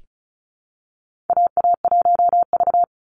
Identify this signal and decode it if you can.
Morse code, heard as a keyed tone, about 36 wpm, 700 Hz. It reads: AA1V
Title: Call Signs - 35wpm